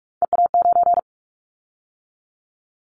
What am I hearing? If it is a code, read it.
Morse code, heard as a keyed tone, about 44 wpm, 700 Hz. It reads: EA9
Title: Callsign Prefixes for Top 100 DXCC - 45wpm